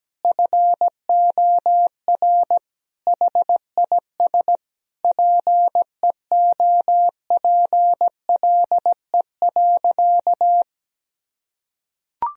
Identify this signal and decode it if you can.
Morse code, heard as a keyed tone, about 17 wpm, 700 Hz. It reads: FOR HIS PEOPLE.
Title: Sentences from Top 1000 Words - Review Part 3 - 17wpm